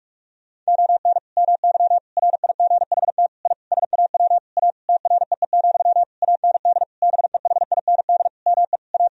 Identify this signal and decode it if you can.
Morse code, heard as a keyed tone, about 47 wpm, 700 Hz. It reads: ON MY RIGHT I SAW A TREE, AND BEHIND ME A
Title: Sentences from Top 1000 Words - Review Part 2 - 45wpm